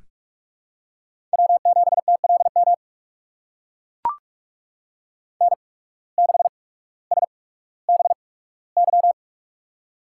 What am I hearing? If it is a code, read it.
Morse code, heard as a keyed tone, about 44 wpm, spaced as for 19 wpm, 690 Hz. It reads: W7TLM N6SBY
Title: USA Call Signs - 8x Character Spacing - 45wpm